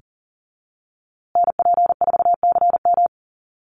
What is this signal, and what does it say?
Morse code, heard as a keyed tone, about 40 wpm, 700 Hz. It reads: NP4CM
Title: USA Call Signs Repeated 3x 40wpm